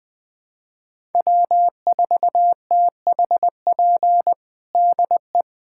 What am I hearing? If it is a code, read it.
Morse code, heard as a keyed tone, about 20 wpm, 700 Hz. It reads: W4THP DE
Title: USA Callsign DE Callsign - 5 Characters - 20wpm